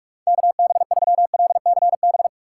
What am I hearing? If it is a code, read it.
Morse code, heard as a keyed tone, about 46 wpm, 700 Hz. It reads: KB3LCB
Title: USA Call Signs - 6 Characters - 3x Character Spacing - 45wpm